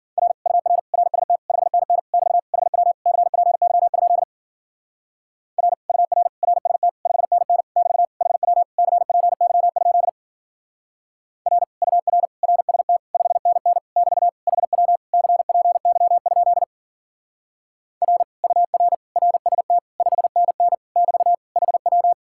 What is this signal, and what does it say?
Morse code, heard as a keyed tone, about 49 wpm, 700 Hz. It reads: R UR RST 5NN <BT> HW CPY? R UR RST 5NN <BT> HW CPY? R UR RST 5NN <BT> HW CPY? R UR RST 5NN <BT> HW
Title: Speed-Racing Common QSO Phrases Encore - 40wpm